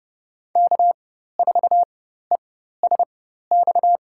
Morse code, heard as a keyed tone, about 30 wpm, spaced as for 14 wpm, 700 Hz. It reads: K4ESX
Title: USA Call Signs - 5 Characters - 4x Character Spacing - 30wpm